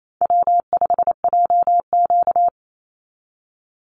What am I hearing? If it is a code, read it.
Morse code, heard as a keyed tone, about 28 wpm, 700 Hz. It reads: W5JQ
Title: Easy Call Signs - 28wpm